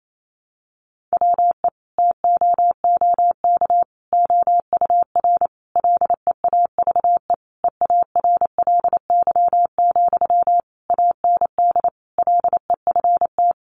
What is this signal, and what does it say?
Morse code, heard as a keyed tone, about 28 wpm, 700 Hz. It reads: WE TOOK OUR LEAVE EARLY, AND LEFT
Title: Sentences from Top 400 Words 28wpm